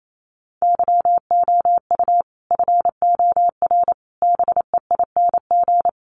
Morse code, heard as a keyed tone, about 28 wpm, 700 Hz. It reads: YOU FOR BEING
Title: Sets of 5 Words 28wpm